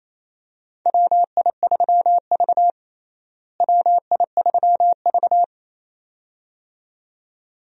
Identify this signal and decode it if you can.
Morse code, heard as a keyed tone, about 28 wpm, 700 Hz. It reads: WI3V WI3V
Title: USA Call Signs - 4 Characters - Repeated 2x - 28wpm